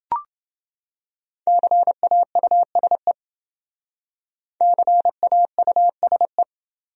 Morse code, heard as a keyed tone, about 28 wpm, 700 Hz. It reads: CAUSE CAUSE